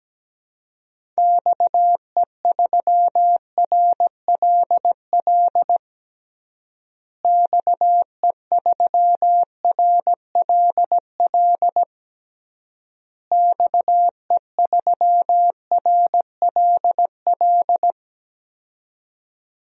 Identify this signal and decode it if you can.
Morse code, heard as a keyed tone, about 17 wpm, 700 Hz. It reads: XE3RLL XE3RLL XE3RLL
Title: DXCC-100 Call Signs - 6 Characters - Repeated 3x - 17wpm